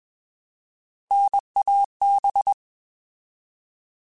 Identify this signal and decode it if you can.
Morse code, heard as a keyed tone, about 20 wpm, 800 Hz. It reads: NAB
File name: Alphabetical / Alphabetical mp3s / Ns @20